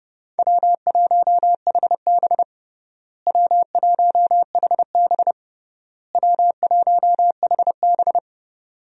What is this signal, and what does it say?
Morse code, heard as a keyed tone, about 32 wpm, 700 Hz. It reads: W1HB W1HB W1HB
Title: USA Call Signs - 4 Characters - Repeated 3x - 30wpm